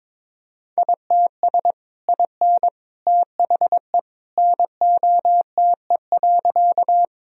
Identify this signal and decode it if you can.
Morse code, heard as a keyed tone, about 22 wpm, 700 Hz. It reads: ITS IN THE NOTE.
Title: Sentences from Top 1000 Words - Review Part 2 - 22wpm